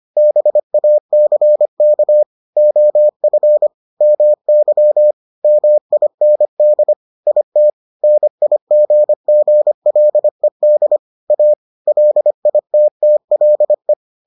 Morse code, heard as a keyed tone, about 25 wpm, 600 Hz. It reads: BACK OF MY MIND IT NIGGLED A LITTLE